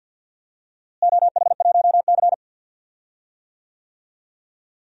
Morse code, heard as a keyed tone, about 52 wpm, 700 Hz. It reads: OH1C
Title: DXCC-100 Call Signs - 4 Characters - Repeated 3x - 50wpm